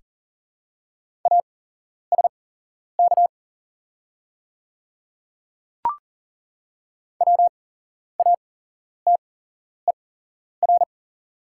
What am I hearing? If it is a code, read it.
Morse code, heard as a keyed tone, about 40 wpm, spaced as for 11 wpm, 700 Hz. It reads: ASK WATER
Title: 8x Character Spacing - Top 200 Words - 40wpm